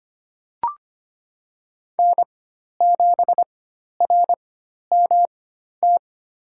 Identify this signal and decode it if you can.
Morse code, heard as a keyed tone, about 25 wpm, spaced as for 12 wpm, 700 Hz. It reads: N7RMT
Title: USA Call Signs - 5 Characters - 4x Character Spacing - 25wpm